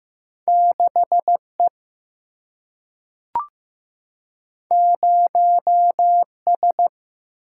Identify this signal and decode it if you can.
Morse code, heard as a keyed tone, about 15 wpm, 700 Hz. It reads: 6E 0S
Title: Single-Letter Single-Number 15wpm